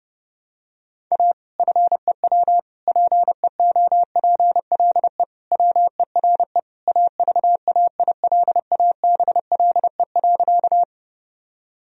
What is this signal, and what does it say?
Morse code, heard as a keyed tone, about 32 wpm, 700 Hz. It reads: A FEW PEOPLE WERE AVAILABLE.
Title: Sentences from Top 1000 Words - Review Part 3 - 30wpm